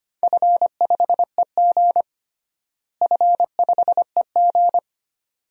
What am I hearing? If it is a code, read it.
Morse code, heard as a keyed tone, about 25 wpm, 700 Hz. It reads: F5EG F5EG
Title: DXCC-100 Call Signs - 4 Characters - Repeated 2x - 25wpm